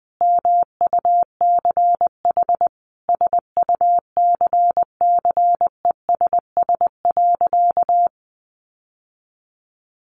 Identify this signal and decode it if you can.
Morse code, heard as a keyed tone, about 20 wpm, 700 Hz. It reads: MUCH SUCCESS.
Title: Sentences from Top 900 Words 20wpm